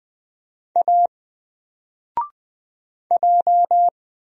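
Morse code, heard as a keyed tone, about 20 wpm, 700 Hz. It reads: A J
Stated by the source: Single letter and number Morse drill at 20 wpm, from N4LCD.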